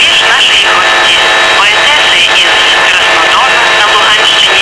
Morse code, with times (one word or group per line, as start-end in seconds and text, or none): none